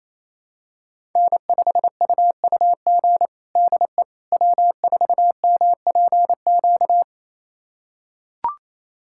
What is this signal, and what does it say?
Morse code, heard as a keyed tone, about 28 wpm, 700 Hz. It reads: N5UUG DE W4MPQ
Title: USA Callsign DE Callsign - 5 Characters - 28wpm